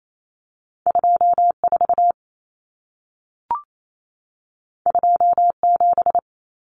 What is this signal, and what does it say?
Morse code, heard as a keyed tone, about 28 wpm, 700 Hz. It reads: 24 27
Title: Numbers - 2-Digit 28wpm